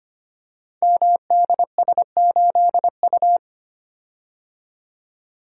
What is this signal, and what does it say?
Morse code, heard as a keyed tone, about 25 wpm, 700 Hz. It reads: MDS8U
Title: Code Groups - 5 Letter:Numbers - 25wpm